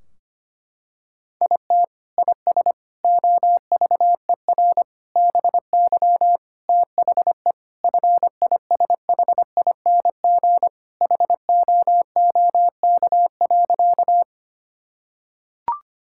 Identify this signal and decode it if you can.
Morse code, heard as a keyed tone, about 25 wpm, 700 Hz. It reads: IT IS OVER BY THE FISHING HOOK.
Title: Sentences from Top 1600 Words 25wpm